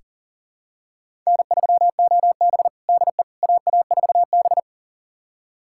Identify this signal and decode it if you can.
Morse code, heard as a keyed tone, about 40 wpm, 700 Hz. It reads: N3OB DE AA4B
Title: USA Callsign DE Callsign - 4 Characters - 40wpm